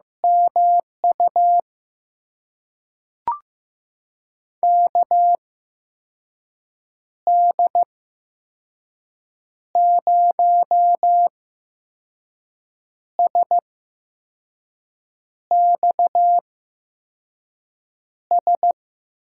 Morse code, heard as a keyed tone, about 15 wpm, spaced as for 5 wpm, 700 Hz. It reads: MU KD0SXS
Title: USA Call Signs - 6 Characters - 8x Character Spacing - 15wpm